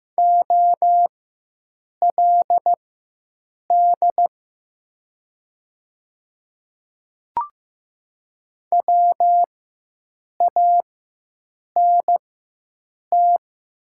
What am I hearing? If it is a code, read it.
Morse code, heard as a keyed tone, about 15 wpm, spaced as for 7 wpm, 700 Hz. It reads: OLD WANT
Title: 4x Character Spacing - Top 500 Words - 15wpm